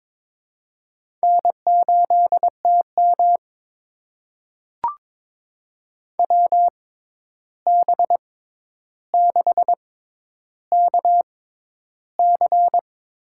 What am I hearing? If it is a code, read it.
Morse code, heard as a keyed tone, about 22 wpm, spaced as for 11 wpm, 700 Hz. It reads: N8TM WB6KC
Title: USA Call Signs - 6x Character Spacing - 22wpm